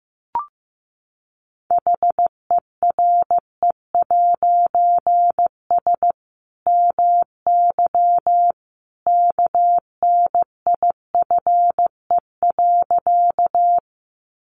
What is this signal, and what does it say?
Morse code, heard as a keyed tone, about 15 wpm, 700 Hz. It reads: HERE'S MY KNIFE.
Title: A Sound Of Thunder 15wpm